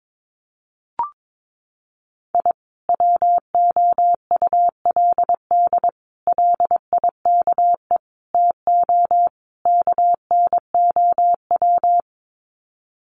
Morse code, heard as a keyed tone, about 22 wpm, 700 Hz. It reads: I WOULD LIKE TO KNOW
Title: Sets of 5 Words Encore 22wpm